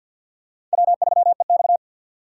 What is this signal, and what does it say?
Morse code, heard as a keyed tone, about 52 wpm, 700 Hz. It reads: W3EX
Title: USA Call Signs - 4 Characters - 6x Character Spacing - 50wpm